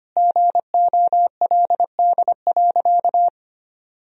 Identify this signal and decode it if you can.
Morse code, heard as a keyed tone, about 25 wpm, 700 Hz. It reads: GOLD.
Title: Sentences from Top 1000 Words - Review Part 3 - 25wpm